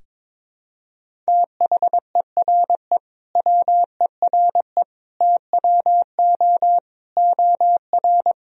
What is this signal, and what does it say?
Morse code, heard as a keyed tone, about 22 wpm, 700 Hz. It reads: THERE WERE TWO OR
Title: Sentences from Top 1000 Words - Review Part 1 - 22wpm